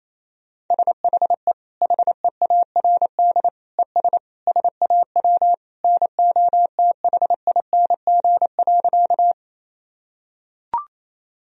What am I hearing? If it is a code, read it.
Morse code, heard as a keyed tone, about 28 wpm, 700 Hz. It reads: SHE HEARD ES SAW NOTHING.
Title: Sentences from Top 500 Words and Abbreviations 28wpm